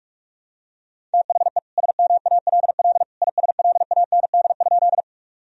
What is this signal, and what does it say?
Morse code, heard as a keyed tone, about 46 wpm, 700 Hz. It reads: THE SMALL ISLAND?
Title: Sentences from Top 1000 Words - Review Part 3 - 45wpm